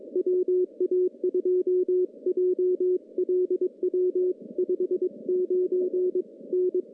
Morse code, heard as a keyed tone, about 22 wpm, 360 Hz. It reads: WA2JLW 59 N
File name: contests / CW-SS-k5tr- -0200